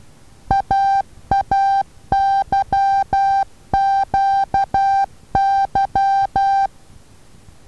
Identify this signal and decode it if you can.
Morse code, heard as a keyed tone, about 12 wpm, 790 Hz. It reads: AAYQY
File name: Taviro HA6VA B